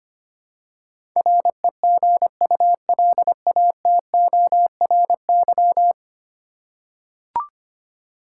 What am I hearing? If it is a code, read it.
Morse code, heard as a keyed tone, about 25 wpm, 700 Hz. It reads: REGULATORY
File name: 10 Letter Words - First 3 Letters Given - 25wpm